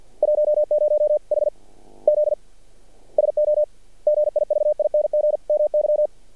Morse code, heard as a keyed tone, about 50 wpm, spaced as for 39 wpm, 600 Hz. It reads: 10 HZ SO GIVING MY
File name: archived / CRSnet